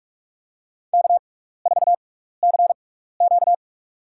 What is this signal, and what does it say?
Morse code, heard as a keyed tone, about 46 wpm, spaced as for 15 wpm, 700 Hz. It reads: K4CQ
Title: USA Call Signs - 4 Characters - 6x Character Spacing - 45wpm